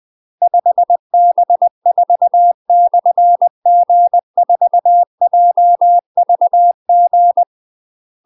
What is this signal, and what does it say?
Morse code, heard as a keyed tone, about 20 wpm, 700 Hz. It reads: 5B4/G4JVG
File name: UK MASTER QSOs TRAINING / 20WPM / UK MASTER QSOs 0035